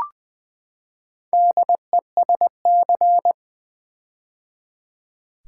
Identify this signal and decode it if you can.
Morse code, heard as a keyed tone, about 20 wpm, 700 Hz. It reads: DESC
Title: Build A Word - 4 Letters At A Time - 11-Letter Words - 20wpm